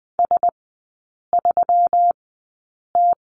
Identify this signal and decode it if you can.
Morse code, heard as a keyed tone, about 20 wpm, spaced as for 8 wpm, 690 Hz. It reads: S3T
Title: 172 Lesson 28e - Number 9 - Three Characters 20wpm